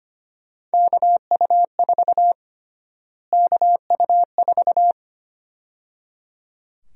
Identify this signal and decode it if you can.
Morse code, heard as a keyed tone, about 25 wpm, 700 Hz. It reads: KU4 KU4
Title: USA Call Sign Prefixes - Repeated 2x - 25wpm